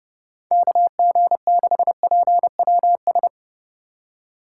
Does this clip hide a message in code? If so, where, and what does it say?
Yes: Morse code from 0.5 to 3.3 s, heard as a keyed tone, about 29 wpm, 700 Hz. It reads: KG6PWS